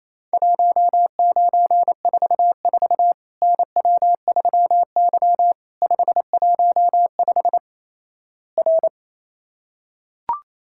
Morse code, heard as a keyed tone, about 28 wpm, 700 Hz. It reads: 1944 NW3Y 515 R
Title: Worked All Europe DX Contest - QTC Exchange - 28wpm